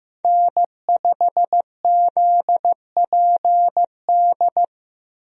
Morse code, heard as a keyed tone, about 15 wpm, 700 Hz. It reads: N5ZPD
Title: USA Call Signs - 5 Characters - Repeated 2x - 15wpm